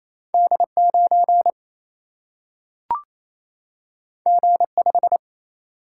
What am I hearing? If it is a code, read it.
Morse code, heard as a keyed tone, about 28 wpm, 700 Hz. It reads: D9 G5
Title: Single-Letter Single-Number 28wpm